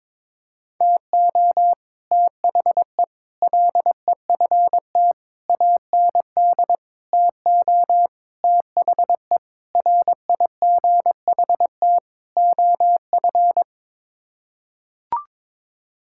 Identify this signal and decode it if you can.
Morse code, heard as a keyed tone, about 21 wpm, 700 Hz. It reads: TO THE LEFT AND TO THE RIGHT OF
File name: practice / 22wpm / Sets of 8 Words - 22wpm